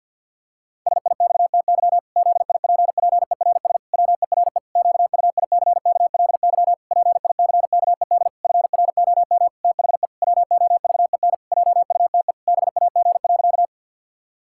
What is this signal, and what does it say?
Morse code, heard as a keyed tone, about 52 wpm, 700 Hz. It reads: SIXTY ZIPPERS WERE QUICKLY PICKED FROM THE WOVEN JUTE BAG.
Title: Pangram Challenge v2 - 50wpm